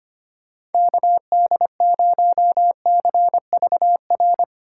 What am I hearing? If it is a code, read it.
Morse code, heard as a keyed tone, about 32 wpm, 700 Hz. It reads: KD0CVR